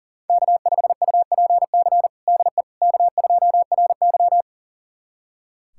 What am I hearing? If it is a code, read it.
Morse code, heard as a keyed tone, about 40 wpm, 700 Hz. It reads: K5UPC DE K2RY
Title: USA Callsign DE Callsign 40wpm